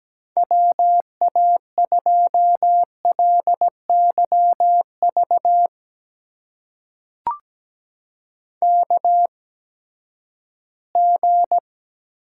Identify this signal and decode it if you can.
Morse code, heard as a keyed tone, about 17 wpm, spaced as for 12 wpm, 700 Hz. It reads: WA2LYV KG
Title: USA Call Signs - 6 Characters - 8x Character Spacing - 17wpm